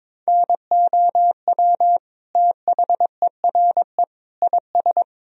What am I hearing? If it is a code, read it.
Morse code, heard as a keyed tone, about 21 wpm, 700 Hz. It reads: NOW THERE IS
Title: Sentences from Top 100 Words and Abbreviations 22wpm